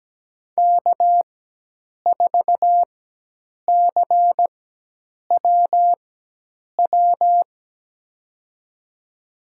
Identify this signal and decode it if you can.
Morse code, heard as a keyed tone, about 17 wpm, spaced as for 8 wpm, 700 Hz. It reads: K4CWW
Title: USA Call Signs - 5 Characters - 4x Character Spacing - 17wpm